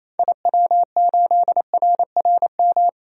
Morse code, heard as a keyed tone, about 28 wpm, 700 Hz. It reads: IW8RRM
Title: DXCC-100 Call Signs - 6 Characters - 28wpm